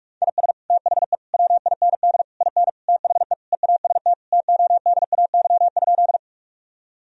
Morse code, heard as a keyed tone, about 46 wpm, 700 Hz. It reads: IS THE WIND IN THE EAST TODAY?
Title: Sentences from Top 1000 Words - Review Part 3 - 45wpm